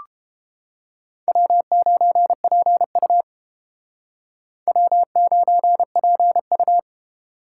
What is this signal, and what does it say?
Morse code, heard as a keyed tone, about 30 wpm, 700 Hz. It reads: W9PU W9PU